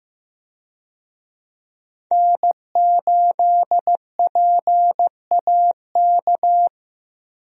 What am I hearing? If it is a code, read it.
Morse code, heard as a keyed tone, about 15 wpm, 700 Hz. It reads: N8PAK
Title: USA Call Signs - 5 Characters - 15wpm